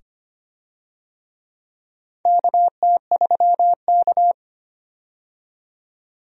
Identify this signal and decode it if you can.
Morse code, heard as a keyed tone, about 25 wpm, 700 Hz. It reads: KT3K
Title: USA Call Signs - 4 Characters - 25wpm